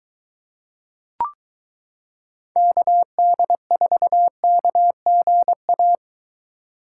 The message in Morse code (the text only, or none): KD4KGA